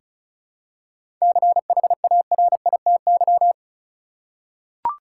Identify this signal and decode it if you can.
Morse code, heard as a keyed tone, about 36 wpm, 700 Hz. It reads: CHARITY
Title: Seven Letter Words - 35wpm